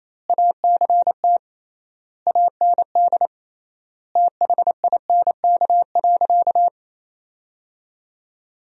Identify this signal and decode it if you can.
Morse code, heard as a keyed tone, about 28 wpm, 700 Hz. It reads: ACT AND THINK.
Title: 3x Word-Spacing - Sentences from Top 500 Words - 28wpm